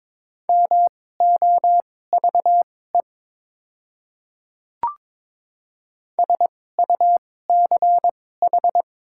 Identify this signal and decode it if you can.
Morse code, heard as a keyed tone, about 22 wpm, spaced as for 16 wpm, 700 Hz. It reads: MOVE SUCH
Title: 2x Character Spacing - Top 200 Words - 22wpm